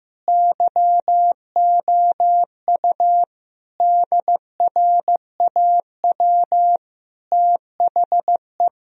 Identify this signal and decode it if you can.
Morse code, heard as a keyed tone, about 15 wpm, 700 Hz. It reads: YOU DRAW THE
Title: Sentences from Top 1000 Words - Review Part 1 - 15wpm